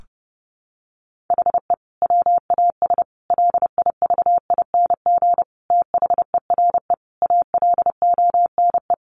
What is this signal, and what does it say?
Morse code, heard as a keyed tone, about 30 wpm, 700 Hz. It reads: HE WAS LIVING THERE ALONE
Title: Sentences from Top 1100 Words 30wpm